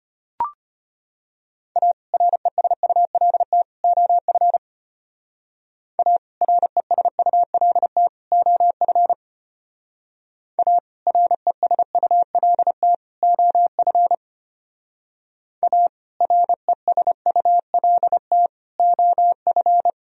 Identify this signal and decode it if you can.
Morse code, heard as a keyed tone, about 32 wpm, 700 Hz. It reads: A RESULT OF A RESULT OF A RESULT OF A RESULT OF